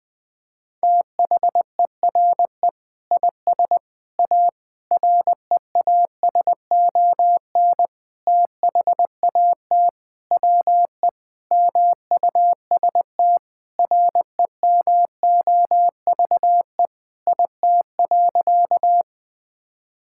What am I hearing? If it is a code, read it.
Morse code, heard as a keyed tone, about 21 wpm, 700 Hz. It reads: THERE IS A REASON THAT WE MUST REMOVE IT.
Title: Sentences from Top 1000 Words - Review Part 5 - 20wpm